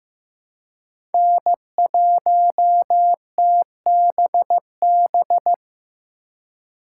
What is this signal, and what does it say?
Morse code, heard as a keyed tone, about 15 wpm, 700 Hz. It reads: N1TBB
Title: USA Call Signs - 5 Characters - 3x Character Spacing - 15wpm